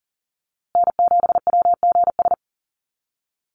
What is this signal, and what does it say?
Morse code, heard as a keyed tone, about 39 wpm, 700 Hz. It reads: N7WGS
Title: USA Call Signs Repeated 3x 40wpm